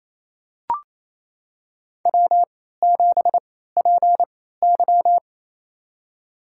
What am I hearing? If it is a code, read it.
Morse code, heard as a keyed tone, about 28 wpm, spaced as for 16 wpm, 700 Hz. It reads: W7PY
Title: USA Call Signs - 4 Characters - 3x Character Spacing - 28wpm